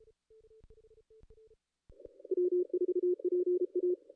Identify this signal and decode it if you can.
Morse code, heard as a keyed tone, about 32 wpm, 360 Hz. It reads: W4PA
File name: contests / ss-cw-k5tr-op-n5rz -2200